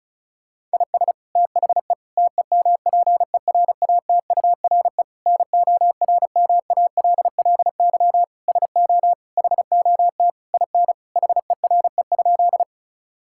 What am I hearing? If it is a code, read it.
Morse code, heard as a keyed tone, about 36 wpm, 700 Hz. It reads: IS THE TEMPERATURE NORMALLY SO HOT IN HERE?
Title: Sentences from Top 1000 Words - Review Part 6 - 35wpm